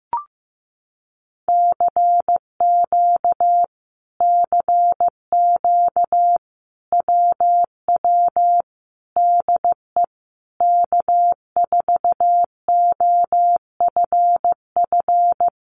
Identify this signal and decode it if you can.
Morse code, heard as a keyed tone, about 15 wpm, 700 Hz. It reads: CQ CQ WW DE K4OFF
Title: Common QSO Phrases Encore 15wpm